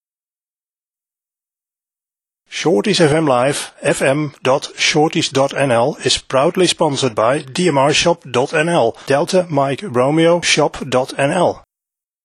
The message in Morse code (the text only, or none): none